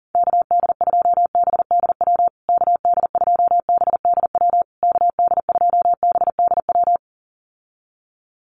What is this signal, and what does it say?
Morse code, heard as a keyed tone, about 40 wpm, 700 Hz. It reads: KD2BDW KD2BDW KD2BDW
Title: USA Call Signs Repeated 3x 40wpm